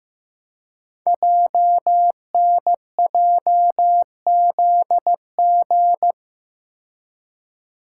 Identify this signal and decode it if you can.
Morse code, heard as a keyed tone, about 15 wpm, 700 Hz. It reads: JNJZG
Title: Code Groups - 5 Letters - 15wpm